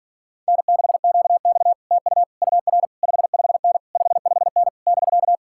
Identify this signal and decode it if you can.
Morse code, heard as a keyed tone, about 47 wpm, 700 Hz. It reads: N6QX TU UR 55N 55N <BK>